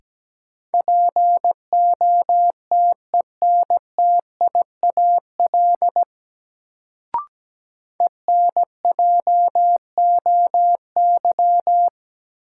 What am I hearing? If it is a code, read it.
Morse code, heard as a keyed tone, about 17 wpm, 700 Hz. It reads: POTENTIAL ENJOY
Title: Sentences from Top 1000 Words - Review Part 4 - New Word Drill - 17wpm